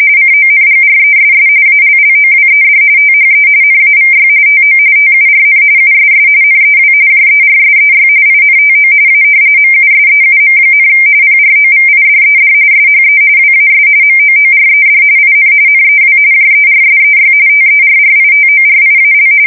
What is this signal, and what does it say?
RTTY, about 45.45 baud. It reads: CONGRESS ALLOWED THE NAVY TO TURN USS NEW JERSEY INTO A MUSEUM, PROVIDED SHE WAS MOORED IN NEW JERSEY TERRITORIAL
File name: fsk170 / battleshipnj